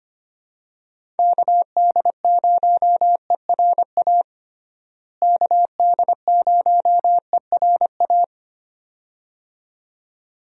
Morse code, heard as a keyed tone, about 25 wpm, 700 Hz. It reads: KD0ERA KD0ERA
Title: USA Call Signs - 6 Characters - Repeated 2x - 25wpm